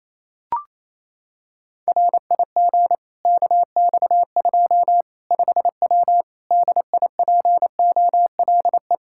Morse code, heard as a keyed tone, about 28 wpm, 700 Hz. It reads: RIG KX2 5W DIPOLE
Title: Practice QSO - Exchange 2 - Info - 28wpm